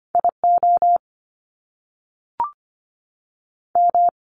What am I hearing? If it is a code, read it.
Morse code, heard as a keyed tone, about 25 wpm, 700 Hz. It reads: IO M
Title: Top Two Letter Combinations in Words 25wpm